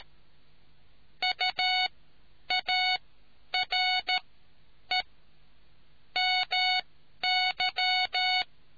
Morse code, heard as a keyed tone, about 13 wpm, spaced as for 9 wpm, 2200 Hz. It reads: UARE MY